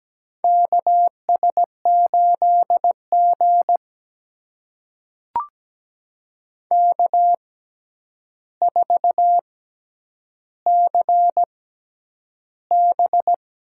Morse code, heard as a keyed tone, about 17 wpm, spaced as for 9 wpm, 700 Hz. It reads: KS8G K4CB
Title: USA Call Signs - 4 Characters - 6x Character Spacing - 17wpm